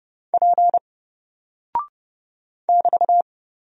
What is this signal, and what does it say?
Morse code, heard as a keyed tone, about 30 wpm, 700 Hz. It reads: P <BT>
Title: Single Character (Letter:Number:Punctuation:Prosign) Alt 30wpm